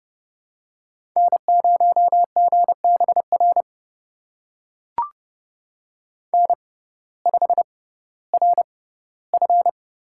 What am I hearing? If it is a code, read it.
Morse code, heard as a keyed tone, about 30 wpm, spaced as for 17 wpm, 700 Hz. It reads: N0GBR N5RF
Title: USA Call Signs - 5 Characters - 6x Character Spacing - 30wpm